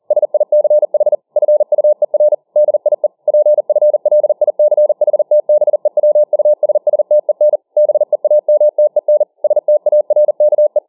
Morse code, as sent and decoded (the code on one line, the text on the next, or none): ... .. -.-. .... / ..-. ..- . .-. / -.. .. / .--. ..-. .-.. .. -.-. .... - -... .-- ..- ... ... - . -. / -... .- -- - . -. / ... - .- .-. -.- .
SICH FUER DIE PFLICHTBEWUSSTEN BEAMTEN STARKE